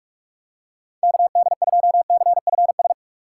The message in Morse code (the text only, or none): KD2CFS